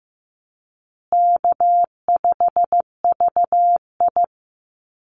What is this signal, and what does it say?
Morse code, heard as a keyed tone, about 15 wpm, 700 Hz. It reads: K5VI
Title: Easy Call Signs Repeated 2x - 15wpm